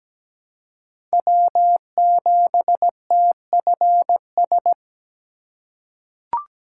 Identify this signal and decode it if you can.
Morse code, heard as a keyed tone, about 17 wpm, 700 Hz. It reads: W7TFS
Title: USA Call Signs - 2x Character Spacing - 17wpm